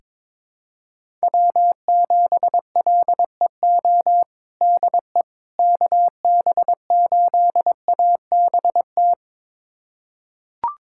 Morse code, heard as a keyed tone, about 22 wpm, 700 Hz. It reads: W7LEO DE KB8ABT
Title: USA Callsign DE Callsign 22wpm